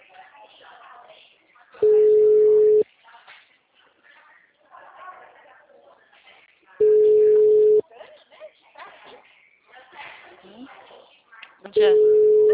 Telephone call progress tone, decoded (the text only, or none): ringback tone